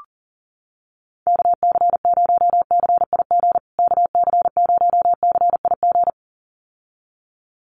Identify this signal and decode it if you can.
Morse code, heard as a keyed tone, about 37 wpm, 700 Hz. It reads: KC0CIG KC0CIG
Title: USA Call Signs Repeated 2x 40wpm